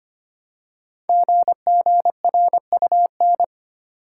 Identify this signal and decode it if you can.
Morse code, heard as a keyed tone, about 25 wpm, 700 Hz. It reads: GGRUN
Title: Code Groups - 5 Letters - 25wpm